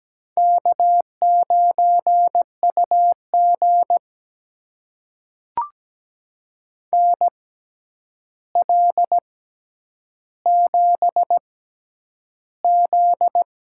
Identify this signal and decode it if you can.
Morse code, heard as a keyed tone, about 17 wpm, spaced as for 9 wpm, 700 Hz. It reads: K9UG NL7Z
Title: USA Call Signs - 4 Characters - 6x Character Spacing - 17wpm